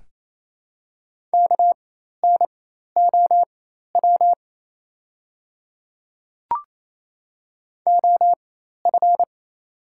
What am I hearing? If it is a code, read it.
Morse code, heard as a keyed tone, about 27 wpm, spaced as for 13 wpm, 690 Hz. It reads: KNOW OF
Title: 4x Character Spacing - Top 200 Words - 28wpm